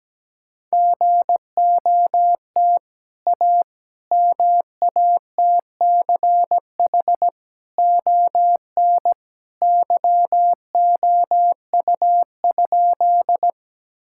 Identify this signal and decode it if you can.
Morse code, heard as a keyed tone, about 17 wpm, 700 Hz. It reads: GOT A MATCH ON YOU?
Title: Sentences from Top 1600 Words 17wpm